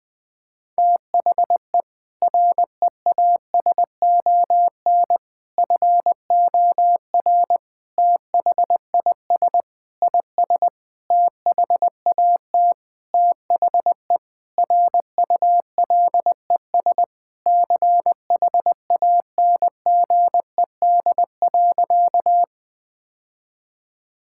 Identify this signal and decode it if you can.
Morse code, heard as a keyed tone, about 20 wpm, 700 Hz. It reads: THE REASON FOR THIS IS THAT THE RULES CHANGED.
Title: Sentences with 7-Word Phrases - 20wpm